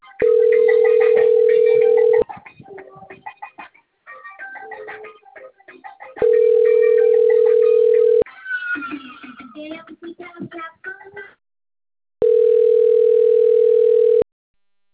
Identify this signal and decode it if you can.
Ringback tone (North America), 440 + 480 Hz.